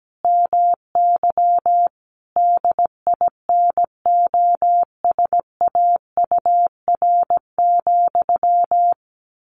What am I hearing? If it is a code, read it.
Morse code, heard as a keyed tone, about 17 wpm, 700 Hz. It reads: MY DINOSAUR,
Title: A Sound Of Thunder 17wpm